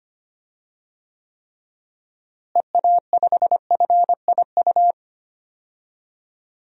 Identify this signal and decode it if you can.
Morse code, heard as a keyed tone, about 25 wpm, 700 Hz. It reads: EA5FIU
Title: DXCC-100 Call Signs - 6 Characters - Repeated 2x - 25wpm